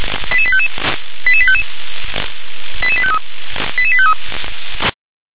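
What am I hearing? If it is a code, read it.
Five-tone selective call (ZVEI), heard as quick runs of five tones, tones of about 70 ms. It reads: 88855 88855 88853 88853
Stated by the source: Five-tone ZVEI selective call 88085, received by SDR.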